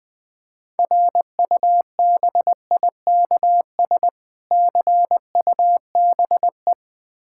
Morse code, heard as a keyed tone, about 20 wpm, 700 Hz. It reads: RUBIKS CUBE
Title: Difficult Words with Hints - 20wpm